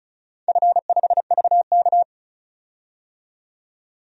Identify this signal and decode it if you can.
Morse code, heard as a keyed tone, about 36 wpm, 700 Hz. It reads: F5VK
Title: DXCC-100 Call Signs - 4 Characters - 35wpm